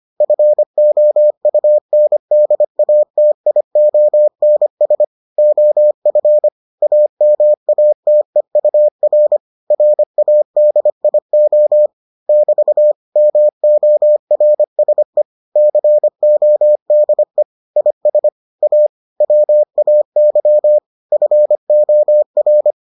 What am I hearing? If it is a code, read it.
Morse code, heard as a keyed tone, about 25 wpm, 600 Hz. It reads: FOUNDATIONS OF AMATEUR RADIO <BT> MORSE CODE IS A WAY FOR